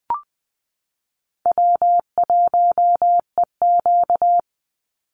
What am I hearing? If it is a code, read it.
Morse code, heard as a keyed tone, about 20 wpm, 700 Hz. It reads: W1EQ
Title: Easy Call Signs - 20wpm